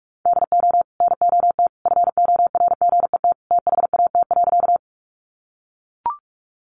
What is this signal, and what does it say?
Morse code, heard as a keyed tone, about 44 wpm, 700 Hz. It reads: DO NOT FORGET THAT.
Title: Sentences from Top 1200 Words 45wpm